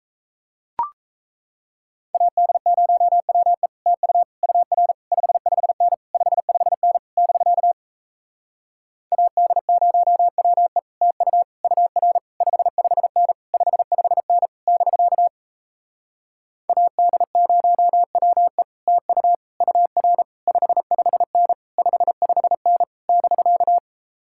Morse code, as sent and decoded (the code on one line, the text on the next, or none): .- -.. ----- .-- . / - ..- / ..- .-. / ..... ..... -. / ..... ..... -. / -...-.- / .- -.. ----- .-- . / - ..- / ..- .-. / ..... ..... -. / ..... ..... -. / -...-.- / .- -.. ----- .-- . / - ..- / ..- .-. / ..... ..... -. / ..... ..... -. / -...-.-
AD0WE TU UR 55N 55N <BK> AD0WE TU UR 55N 55N <BK> AD0WE TU UR 55N 55N <BK>